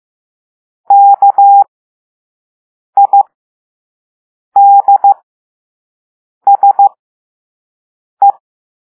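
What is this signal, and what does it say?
Morse code, heard as a keyed tone, about 15 wpm, spaced as for 5 wpm, 800 Hz. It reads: KIDSE
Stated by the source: Morse code lesson 5 drill of random five-letter groups using all letters learned so far.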